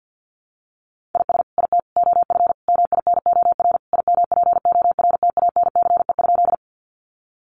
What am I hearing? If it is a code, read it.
Morse code, heard as a keyed tone, about 50 wpm, 700 Hz. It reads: IS IT OF MINOR IMPORTANCE?
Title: Sentences from Top 1500 Words 50wpm